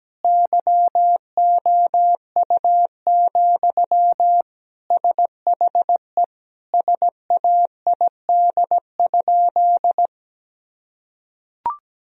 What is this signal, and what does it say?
Morse code, heard as a keyed tone, about 17 wpm, 700 Hz. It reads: YOU, SHE SAID?
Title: Sentences from Top 1000 Words - Review Part 1 - 17wpm